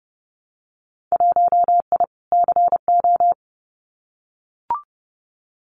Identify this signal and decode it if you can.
Morse code, heard as a keyed tone, about 30 wpm, 700 Hz. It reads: 1I CO
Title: Winter Field Day Exchanges Encore 30wpm